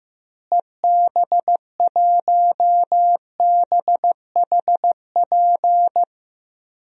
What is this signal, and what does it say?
Morse code, heard as a keyed tone, about 15 wpm, 700 Hz. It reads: EB1BHP
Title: DXCC-100 Call Signs - 6 Characters - Repeated 2x - 15wpm